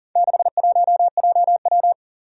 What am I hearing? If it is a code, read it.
Morse code, heard as a keyed tone, about 41 wpm, 700 Hz. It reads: 61JW
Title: Four Challenging Letter-Number Combinations 40wpm